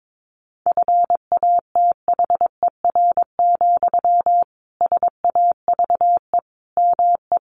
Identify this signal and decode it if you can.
Morse code, heard as a keyed tone, about 22 wpm, 700 Hz. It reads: FATHER, SAVE ME
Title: Sentences from Top 1100 Words 22wpm